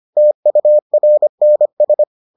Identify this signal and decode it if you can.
Morse code, heard as a keyed tone, about 25 wpm, 600 Hz. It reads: TURNS